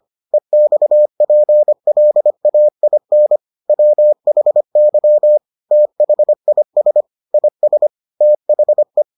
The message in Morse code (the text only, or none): EXPLAIN WHY THIS IS THE